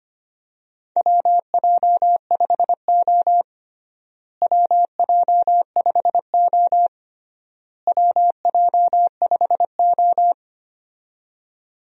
Morse code, heard as a keyed tone, about 25 wpm, 700 Hz. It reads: WJ5O WJ5O WJ5O
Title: USA Call Signs - 4 Characters - Repeated 3x - 25wpm